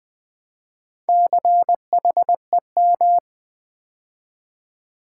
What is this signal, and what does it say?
Morse code, heard as a keyed tone, about 20 wpm, 700 Hz. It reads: CHEM
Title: Build A Word - 2 Letters At A Time - 8-Letter Words - 20wpm